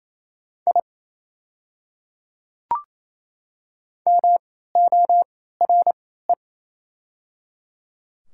Morse code, heard as a keyed tone, about 28 wpm, spaced as for 16 wpm, 690 Hz. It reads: I MORE
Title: 3x Character Spacing - Top 100 Words - 28wpm